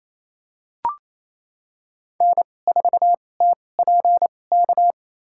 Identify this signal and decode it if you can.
Morse code, heard as a keyed tone, about 28 wpm, spaced as for 20 wpm, 700 Hz. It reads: N4TPK
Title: USA Call Signs - 5 Characters - 2x Character Spacing - 28wpm